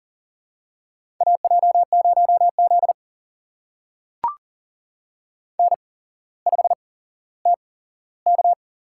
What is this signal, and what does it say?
Morse code, heard as a keyed tone, about 40 wpm, spaced as for 17 wpm, 700 Hz. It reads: AJ0Z N5TK
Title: USA Call Signs - 4 Characters - 8x Character Spacing - 40wpm